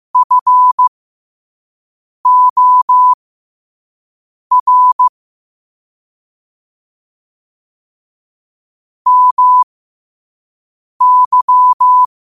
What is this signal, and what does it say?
Morse code, heard as a keyed tone, about 15 wpm, spaced as for 5 wpm, 1000 Hz. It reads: FOR MY
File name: qotdfiles / 5wpm